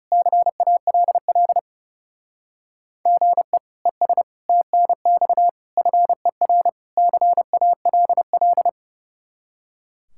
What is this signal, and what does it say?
Morse code, heard as a keyed tone, about 37 wpm, 700 Hz. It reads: CALL GE ES TNX FER CALL
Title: Speed-Racing Common QSO Phrases Encore - 30wpm